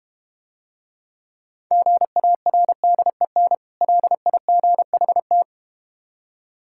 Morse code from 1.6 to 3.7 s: GARDEN

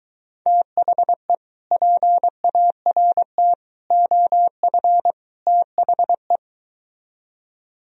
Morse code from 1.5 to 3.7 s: PART

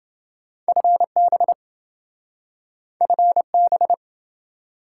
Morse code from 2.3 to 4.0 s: FB

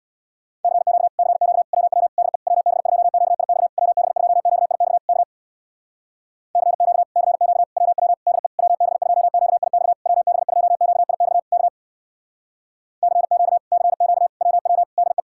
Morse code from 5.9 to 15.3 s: CQ CQ WW DE WB2QIX WB2QIX K CQ CQ WW DE

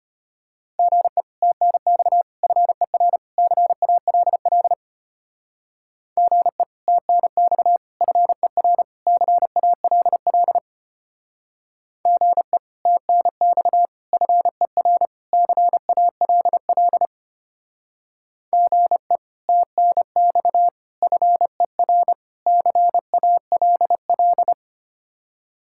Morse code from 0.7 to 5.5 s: GE TNX FER CALL

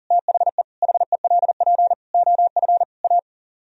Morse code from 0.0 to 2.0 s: THE HELP

